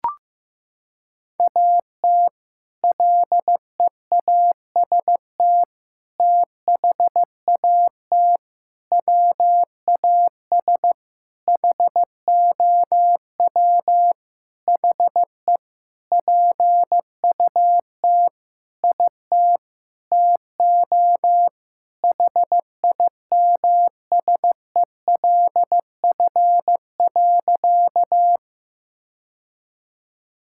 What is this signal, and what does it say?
Morse code, heard as a keyed tone, about 15 wpm, 700 Hz. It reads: AT LEAST THAT WAS HOW HE PUT IT TO HIMSELF.
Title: Sentences from Top 1600 Words 15wpm